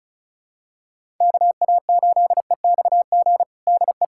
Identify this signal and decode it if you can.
Morse code, heard as a keyed tone, about 36 wpm, 700 Hz. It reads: KA8EXG DE
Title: USA Callsign DE Callsign 35wpm